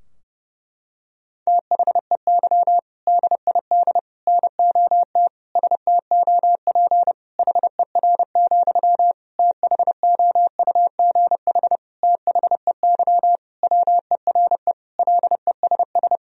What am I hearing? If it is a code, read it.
Morse code, heard as a keyed tone, about 32 wpm, 700 Hz. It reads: THEY DID NOT STOP HER, THOUGH THEY WERE LESS
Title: Sentences from Top 1000 Words - Review Part 2 - 30wpm